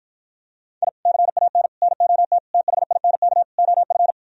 Morse code, heard as a keyed tone, about 52 wpm, 700 Hz. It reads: I CAN NOT THINK OF